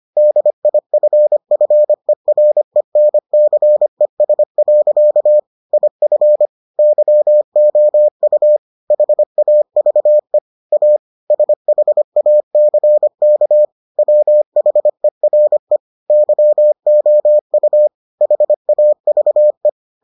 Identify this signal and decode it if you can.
Morse code, heard as a keyed tone, about 25 wpm, 600 Hz. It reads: DIFFERENCES. IF YOU HAVE A SHACK WHERE YOU HAVE